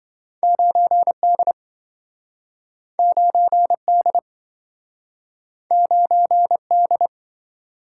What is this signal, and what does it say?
Morse code, heard as a keyed tone, about 28 wpm, 700 Hz. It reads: 9D 9D 9D